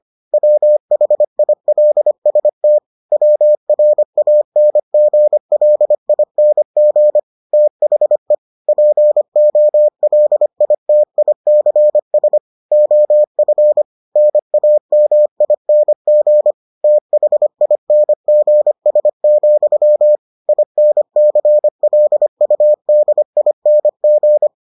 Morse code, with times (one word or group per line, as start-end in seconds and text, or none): 0.3-2.8 WHILST
3.1-7.2 WRANGLING
7.5-8.3 THE
8.7-12.4 POLITICS
12.7-13.8 OF
14.2-16.5 NAMING
16.8-20.2 THINGS,
20.5-24.5 INCLUDING